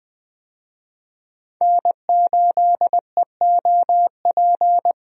Morse code, heard as a keyed tone, about 20 wpm, 700 Hz. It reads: N8EOP
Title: USA Call Signs - 5 Characters - 20wpm